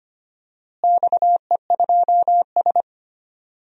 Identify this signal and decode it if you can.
Morse code, heard as a keyed tone, about 25 wpm, 700 Hz. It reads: XE2S
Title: DXCC-100 Call Signs - 4 Characters - Repeated 3x - 25wpm